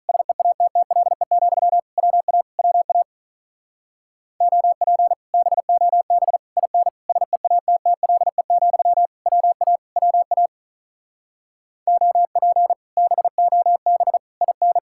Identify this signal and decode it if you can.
Morse code, heard as a keyed tone, about 46 wpm, 700 Hz. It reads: SEATTLE, WA WA OP BOB IN SEATTLE, WA WA OP BOB IN